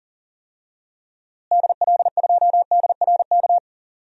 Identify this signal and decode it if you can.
Morse code, heard as a keyed tone, about 40 wpm, 700 Hz. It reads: DL2DRK
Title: DXCC-100 Call Signs - 6 Characters - Repeated 2x - 40wpm